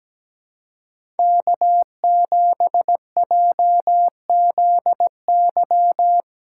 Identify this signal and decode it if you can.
Morse code, heard as a keyed tone, about 17 wpm, 700 Hz. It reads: K7JZY
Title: USA Call Signs - 5 Characters - 4x Character Spacing - 17wpm